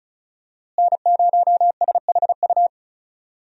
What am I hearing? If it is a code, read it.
Morse code, heard as a keyed tone, about 34 wpm, 700 Hz. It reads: N0SHU
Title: USA Call Signs - 5 Characters - Repeated 2x - 35wpm